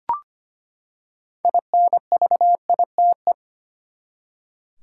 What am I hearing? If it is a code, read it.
Morse code, heard as a keyed tone, about 25 wpm, 700 Hz. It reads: INVITE
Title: Six Letter Words - 25wpm